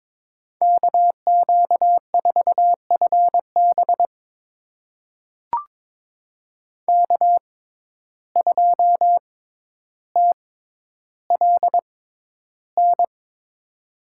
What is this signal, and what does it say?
Morse code, heard as a keyed tone, about 22 wpm, spaced as for 11 wpm, 700 Hz. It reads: KQ4FB K2TLN
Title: USA Call Signs - 5 Characters - 6x Character Spacing - 22wpm